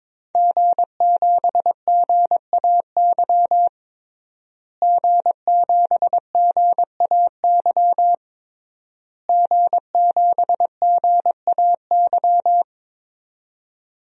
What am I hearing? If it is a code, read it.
Morse code, heard as a keyed tone, about 22 wpm, 700 Hz. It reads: G7GAY G7GAY G7GAY
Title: DXCC-100 Call Signs - 5 Characters - Repeated 3x - 22wpm